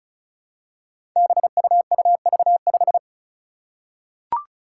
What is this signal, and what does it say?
Morse code, heard as a keyed tone, about 36 wpm, 700 Hz. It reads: BUUV5
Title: Five Challenging Letter-Number Combinations 35wpm